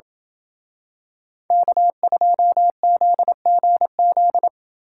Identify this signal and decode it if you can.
Morse code, heard as a keyed tone, about 27 wpm, 700 Hz. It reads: K2ZGZ